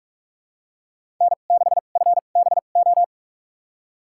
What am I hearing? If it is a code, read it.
Morse code, heard as a keyed tone, about 46 wpm, spaced as for 33 wpm, 700 Hz. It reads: N6FBO
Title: USA Call Signs - 5 Characters - 2x Character Spacing - 45wpm